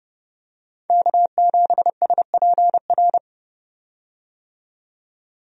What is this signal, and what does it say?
Morse code, heard as a keyed tone, about 30 wpm, 700 Hz. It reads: K7SPR
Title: USA Call Signs - 5 Characters - 30wpm